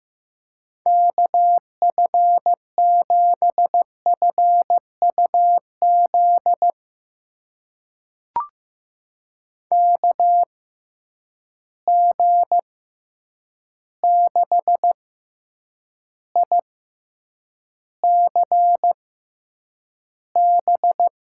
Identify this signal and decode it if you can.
Morse code, heard as a keyed tone, about 15 wpm, spaced as for 8 wpm, 700 Hz. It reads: KF7FUZ KG6ICB
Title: USA Call Signs - 6 Characters - 6x Character Spacing - 15wpm